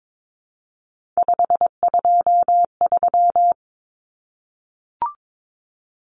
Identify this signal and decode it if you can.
Morse code, heard as a keyed tone, about 22 wpm, 700 Hz. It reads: 523
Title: Numbers - 3-Digit 22wpm